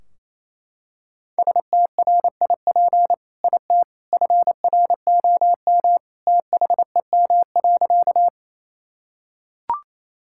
Morse code, heard as a keyed tone, about 28 wpm, 700 Hz. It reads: STRIP IT FROM THEM.
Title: Sentences from Top 1600 Words 28wpm